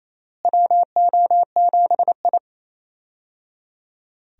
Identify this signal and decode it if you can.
Morse code, heard as a keyed tone, about 28 wpm, 700 Hz. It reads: WO7I
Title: USA Call Signs - 4 Characters - Repeated 3x - 28wpm